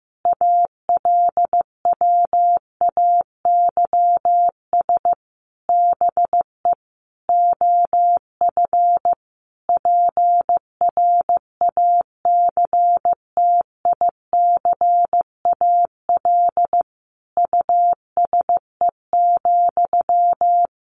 Morse code, heard as a keyed tone, about 15 wpm, 700 Hz. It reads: ALWAYS BE OF PRACTICAL USE,
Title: Sentences from Top 1400 Words 15wpm